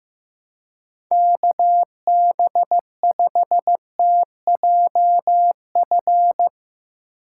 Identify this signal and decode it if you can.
Morse code, heard as a keyed tone, about 16 wpm, 700 Hz. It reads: KB5TJF
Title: Speed Racing - USA 2x3 Call Signs 15wpm